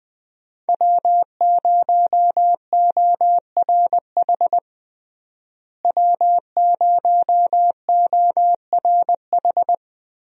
Morse code, heard as a keyed tone, about 20 wpm, 700 Hz. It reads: W0ORH W0ORH
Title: USA Call Signs - 5 Characters - Repeated 3x - 20wpm